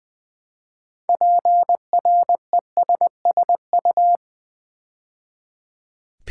Morse code, heard as a keyed tone, about 20 wpm, 700 Hz. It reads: PRESSU
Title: Build A Word - 3 Letters At A Time - 8-Letter Words - 20wpm